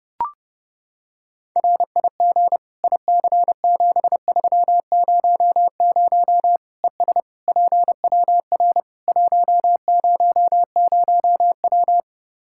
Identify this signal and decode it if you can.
Morse code, heard as a keyed tone, about 31 wpm, 700 Hz. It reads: RIG IC7300 ES PWR 100W
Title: Practice QSO - Exchange 2 - Info - 30wpm